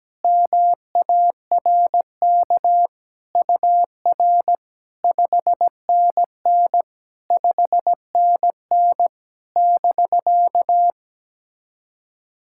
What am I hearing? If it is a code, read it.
Morse code, heard as a keyed tone, about 18 wpm, 700 Hz. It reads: MARK UR 5NN 5NN <BK>
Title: SOTA QSO Practice 17wpm